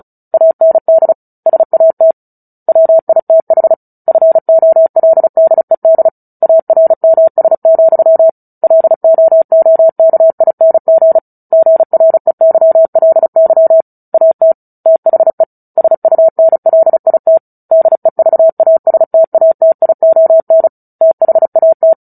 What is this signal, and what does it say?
Morse code, heard as a keyed tone, about 35 wpm, 650 Hz. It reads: AND SAT WITH FOLDED ARMS, LOOKING GREYLY AT THE SUNLIT DEVASTATION THAT
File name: Book Two / 09-Chapter 9 - 35 WPM